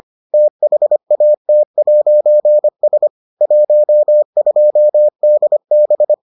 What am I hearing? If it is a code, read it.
Morse code, heard as a keyed tone, about 25 wpm, 600 Hz. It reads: THAT'S 12DB